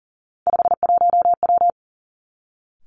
Morse code, heard as a keyed tone, about 37 wpm, 700 Hz. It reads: 51W
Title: Three Challenging Letter-Number Combinations 40wpm